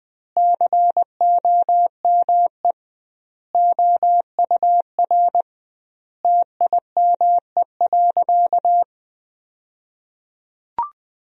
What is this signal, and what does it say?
Morse code, heard as a keyed tone, about 21 wpm, 700 Hz. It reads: COME OUR TIME.
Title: 2x Word-Spacing - Sentences from Top 200 Words - 20wpm